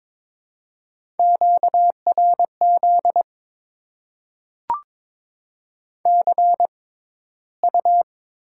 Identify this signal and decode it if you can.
Morse code, heard as a keyed tone, about 22 wpm, spaced as for 14 wpm, 700 Hz. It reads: QRZ CU
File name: Top 90 QSO Elements - 6x Character Spacing - 22wpm